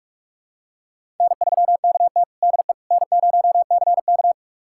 Morse code, heard as a keyed tone, about 46 wpm, 700 Hz. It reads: N3KT DE N0CK
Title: USA Callsign DE Callsign - 4 Characters - 45wpm